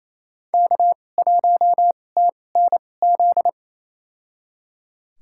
Morse code, heard as a keyed tone, about 28 wpm, spaced as for 20 wpm, 690 Hz. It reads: K1TNZ
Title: USA Call Signs - 2x Character Spacing - 28wpm